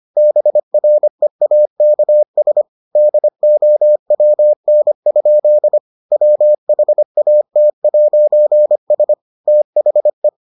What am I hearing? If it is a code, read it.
Morse code, heard as a keyed tone, about 25 wpm, 600 Hz. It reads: BREAKS DOWN? WHAT'S THE